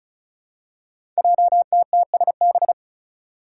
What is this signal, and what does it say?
Morse code, heard as a keyed tone, about 35 wpm, 700 Hz. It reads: JTTSB
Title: Five Challenging Letter-Number Combinations 35wpm